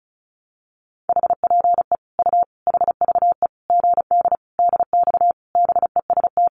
Morse code, heard as a keyed tone, about 34 wpm, 700 Hz. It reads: HPE U HVE GD DX BEST
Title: Common QSO Phrases 35wpm